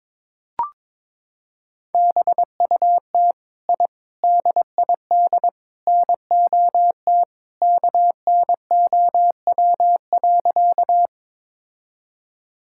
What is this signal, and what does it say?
Morse code, heard as a keyed tone, about 22 wpm, 700 Hz. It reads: BUT I DID NOT KNOW.
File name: practice / 22wpm / Sentences with 4-Word Phrases - 22wpm